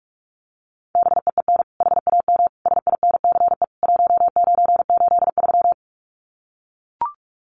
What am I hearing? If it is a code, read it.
Morse code, heard as a keyed tone, about 44 wpm, 700 Hz. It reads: BEEN HAM SINCE 1983
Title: Common QSO Phrases 45wpm